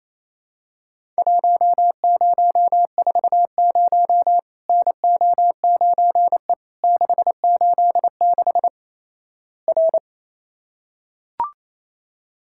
Morse code, heard as a keyed tone, about 28 wpm, 700 Hz. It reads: 1040 NO9E 686 R
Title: Worked All Europe DX Contest - QTC Exchange - 28wpm